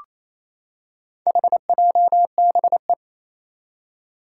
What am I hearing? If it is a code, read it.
Morse code, heard as a keyed tone, about 28 wpm, 700 Hz. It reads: HJBE
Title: Four Challenging Letter-Number Combinations 28wpm